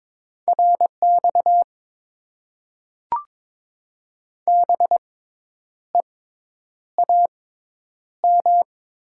Morse code, heard as a keyed tone, about 22 wpm, spaced as for 9 wpm, 700 Hz. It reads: RX BEAM
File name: Top 90 QSO Elements - 6x Character Spacing - 22wpm